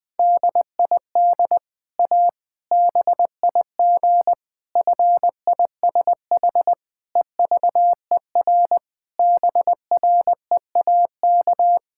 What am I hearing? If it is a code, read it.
Morse code, heard as a keyed tone, about 20 wpm, 700 Hz. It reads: DID A BIG FISH EVER BREAK
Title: Sentences from Top 1000 Words - Review Part 6 - 20wpm